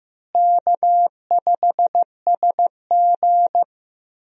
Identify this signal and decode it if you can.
Morse code, heard as a keyed tone, about 15 wpm, 700 Hz. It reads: K5SG
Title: USA Call Signs - 4 Characters - Repeated 3x - 15wpm